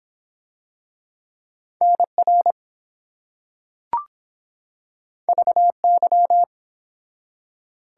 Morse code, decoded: NR VY

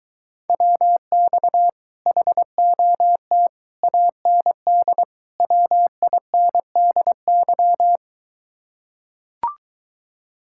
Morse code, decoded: WX HOT AND WINDY

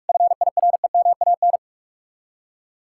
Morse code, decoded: FIREMAN

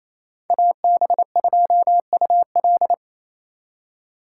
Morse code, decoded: AB2UL

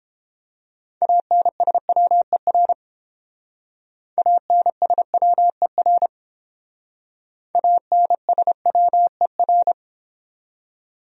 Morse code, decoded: ANSWER ANSWER ANSWER

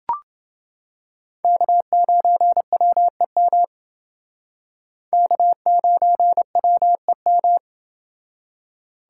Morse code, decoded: K9WEM K9WEM

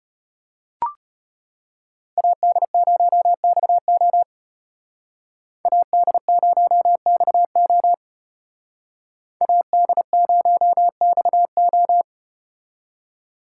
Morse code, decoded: AD0XO AD0XO AD0XO

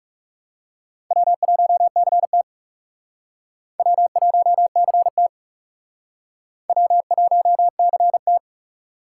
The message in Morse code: W1CT W1CT W1CT

